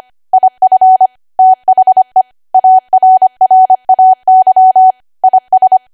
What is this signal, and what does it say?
Morse code, heard as a keyed tone, about 25 wpm, 750 Hz. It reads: IF THE ARRAY IS
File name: Archive / 25 WPM / 25WPM